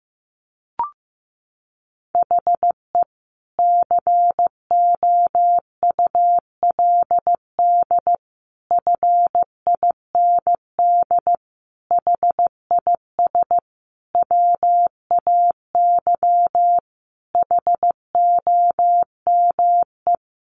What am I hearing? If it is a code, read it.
Morse code, heard as a keyed tone, about 15 wpm, 700 Hz. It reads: HE COULD FIND HIS WAY HOME
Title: Sentences from Top 1400 Words 15wpm